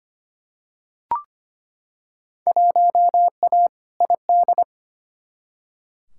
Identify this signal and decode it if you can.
Morse code, heard as a keyed tone, about 25 wpm, 700 Hz. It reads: 1A ID
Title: Field Day Exchanges - 25wpm